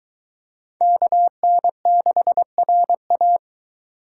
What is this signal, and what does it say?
Morse code, heard as a keyed tone, about 23 wpm, 700 Hz. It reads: KN6RA